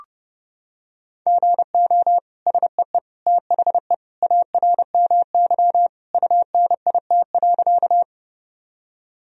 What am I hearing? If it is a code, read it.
Morse code, heard as a keyed tone, about 32 wpm, 700 Hz. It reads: GO SEE THE ARMY UNIT.
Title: Sentences from Top 1000 Words - Review Part 2 - 30wpm